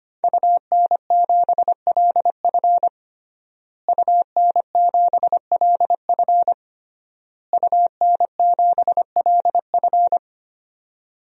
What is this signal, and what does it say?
Morse code, heard as a keyed tone, about 25 wpm, 700 Hz. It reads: UN7LF UN7LF UN7LF
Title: DXCC-100 Call Signs - 5 Characters - Repeated 3x - 25wpm